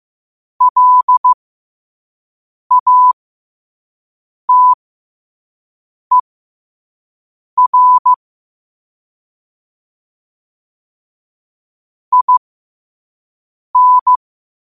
Morse code, heard as a keyed tone, about 15 wpm, spaced as for 5 wpm, 1000 Hz. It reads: LATER IN
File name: qotdfiles / 5wpm